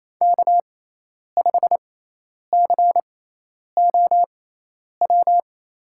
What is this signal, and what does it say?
Morse code, heard as a keyed tone, about 28 wpm, spaced as for 10 wpm, 700 Hz. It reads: K5COW
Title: USA Call Signs - 5 Characters - 6x Character Spacing - 28wpm